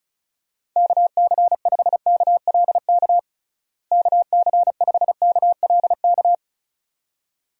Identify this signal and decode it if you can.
Morse code, heard as a keyed tone, about 36 wpm, 700 Hz. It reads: KC5KLK KC5KLK
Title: USA Call Signs - 6 Characters - Repeated 2x - 35wpm